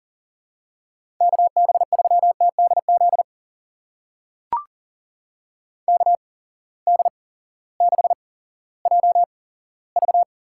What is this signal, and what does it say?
Morse code, heard as a keyed tone, about 41 wpm, spaced as for 18 wpm, 700 Hz. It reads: KB3TDZ KD6JV
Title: USA Call Signs - 6 Characters - 8x Character Spacing - 40wpm